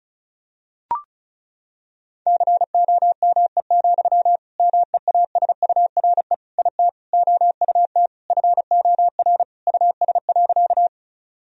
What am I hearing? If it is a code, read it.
Morse code, heard as a keyed tone, about 36 wpm, 700 Hz. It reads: COME, MEASURE IT OUT FOR US.
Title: Sentences from Top 1000 Words - Review Part 2 - 35wpm